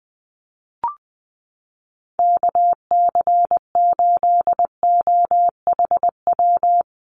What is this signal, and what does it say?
Morse code, heard as a keyed tone, about 20 wpm, 700 Hz. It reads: KC8OHW
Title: USA Call Signs Repeated 3x 20wpm